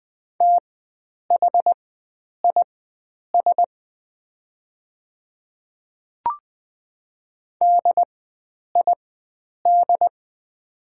Morse code, heard as a keyed tone, about 20 wpm, spaced as for 9 wpm, 700 Hz. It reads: THIS DID
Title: 4x Character Spacing - Top 100 Words - 20wpm